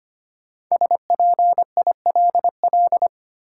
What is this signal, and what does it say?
Morse code, heard as a keyed tone, about 25 wpm, 700 Hz. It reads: SPILL
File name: 5 Letter Words - First 3 Letters Given - 25wpm